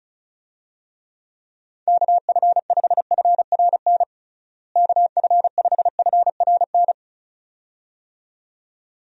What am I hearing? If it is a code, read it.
Morse code, heard as a keyed tone, about 36 wpm, 700 Hz. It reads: KF5FRN KF5FRN
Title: USA Call Signs - 6 Characters - Repeated 2x - 35wpm